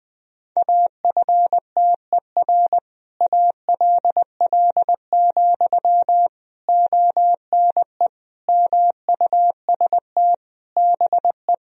Morse code, heard as a keyed tone, about 21 wpm, 700 Hz. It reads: AFTER ALL, ONE MUST BE
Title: Sentences from Top 1000 Words - Review Part 3 - 20wpm